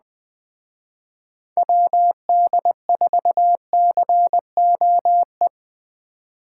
Morse code, heard as a keyed tone, about 20 wpm, 700 Hz. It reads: WD4COE